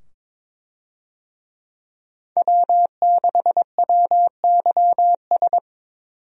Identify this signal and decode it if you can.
Morse code, heard as a keyed tone, about 22 wpm, 700 Hz. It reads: W6WYS
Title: USA Call Signs - 5 Characters - 22wpm